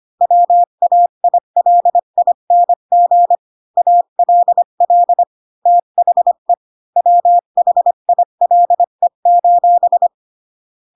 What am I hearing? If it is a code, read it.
Morse code, heard as a keyed tone, about 25 wpm, 700 Hz. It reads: WAILING ALL THE WHILE:
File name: Pinocchio / Pinocchio ch5 20 25 ik1hge